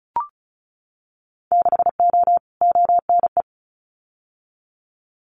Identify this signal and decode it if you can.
Morse code, heard as a keyed tone, about 36 wpm, 700 Hz. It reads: 6O ONE
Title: Winter Field Day Exchanges Encore 35wpm